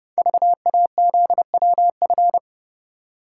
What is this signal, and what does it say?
Morse code, heard as a keyed tone, about 30 wpm, 700 Hz. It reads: VAZWF
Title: Code Groups - 5 Letter:Numbers - 30wpm